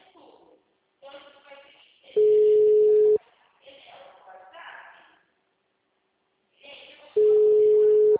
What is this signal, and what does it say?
Ringback tone (Europe, 425 Hz), with speech.